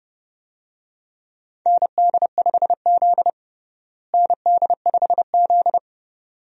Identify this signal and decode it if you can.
Morse code, heard as a keyed tone, about 30 wpm, 700 Hz. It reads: ND5Z ND5Z
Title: USA Call Signs - 4 Characters - Repeated 2x - 30wpm